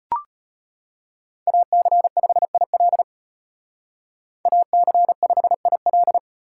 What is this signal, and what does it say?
Morse code, heard as a keyed tone, about 38 wpm, 700 Hz. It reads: AC5IL AC5IL